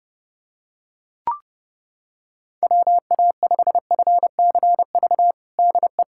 Morse code, heard as a keyed tone, about 30 wpm, 700 Hz. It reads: WA5FCV DE
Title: USA Callsign DE Callsign - 6 Characters - 30wpm